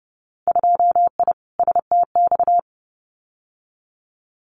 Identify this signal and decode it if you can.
Morse code, heard as a keyed tone, about 30 wpm, 700 Hz. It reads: 2I STX
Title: Winter Field Day Exchanges 30wpm